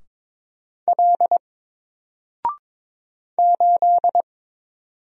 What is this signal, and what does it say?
Morse code, heard as a keyed tone, about 22 wpm, 700 Hz. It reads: L 8
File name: Single-Letter-Number-22wpm